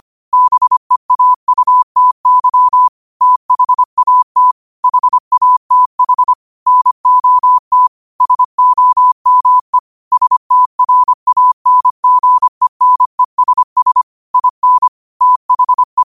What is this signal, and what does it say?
Morse code, heard as a keyed tone, about 25 wpm, 1000 Hz. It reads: BEAUTY THAT HATH NOT SOME STRANGENESS IN THE